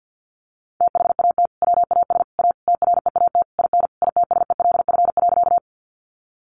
Morse code, heard as a keyed tone, about 50 wpm, 700 Hz. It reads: THAT WAS A TREAT IN ITSELF.
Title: Sentences from Top 1300 Words 50wpm